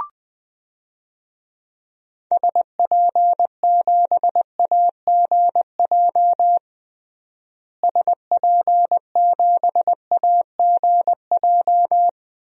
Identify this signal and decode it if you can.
Morse code, heard as a keyed tone, about 20 wpm, 700 Hz. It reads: SP7AGJ SP7AGJ
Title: DXCC-100 Call Signs - 6 Characters - Repeated 2x - 20wpm